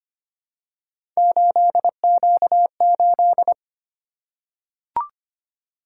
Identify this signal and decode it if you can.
Morse code, heard as a keyed tone, about 25 wpm, 700 Hz. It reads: 8Q8
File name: Three Letter-Number Combinations - 25wpm